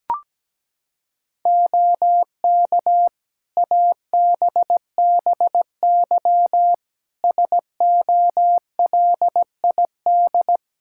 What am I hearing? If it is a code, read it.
Morse code, heard as a keyed tone, about 17 wpm, 700 Hz. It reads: OK ABBY SOLID
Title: Practice QSO - Exchange 3 - More Info - 17wpm